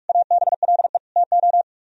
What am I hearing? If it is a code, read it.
Morse code, heard as a keyed tone, about 44 wpm, 700 Hz. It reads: ABLE TO